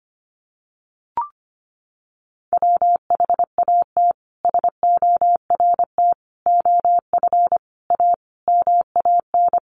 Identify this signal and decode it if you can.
Morse code, heard as a keyed tone, about 25 wpm, 700 Hz. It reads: WHAT SORT OF A MAN
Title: Sentences from Top 1300 Words 25wpm